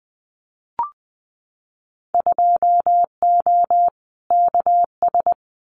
Morse code, heard as a keyed tone, about 20 wpm, 700 Hz. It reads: 2O KS
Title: Winter Field Day Exchanges Encore 20wpm